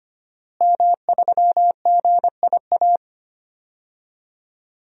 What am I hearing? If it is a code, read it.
Morse code, heard as a keyed tone, about 25 wpm, 700 Hz. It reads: M3GIA
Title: DXCC-100 Call Signs - 5 Characters - Repeated 2x - 25wpm